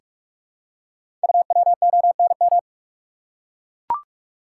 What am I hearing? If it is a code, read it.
Morse code, heard as a keyed tone, about 44 wpm, 700 Hz. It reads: UWONM
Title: Code Groups - 5 Letters - 45wpm